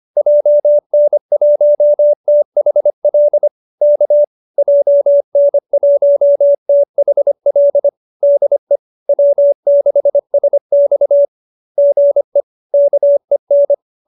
Morse code, heard as a keyed tone, about 25 wpm, 590 Hz. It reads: JN1THL K JN1THL DE W6SX GE KEN
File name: QSO305 25